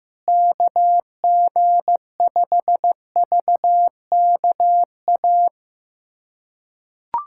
Morse code, heard as a keyed tone, about 15 wpm, 700 Hz. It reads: KG5VKA
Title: USA Callsign DE Callsign - 6 Characters - 15wpm